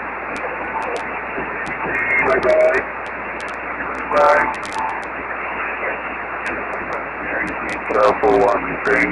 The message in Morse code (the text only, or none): none